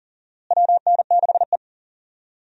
Morse code, heard as a keyed tone, about 40 wpm, 700 Hz. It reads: WN6E
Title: USA Call Signs - 4 Characters - Repeated 2x - 40wpm